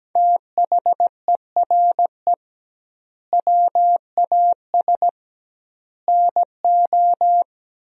Morse code, heard as a keyed tone, about 17 wpm, 700 Hz. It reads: THERE WAS NO
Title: 2x Word-Spacing - Sentences from Top 300 Words - 17wpm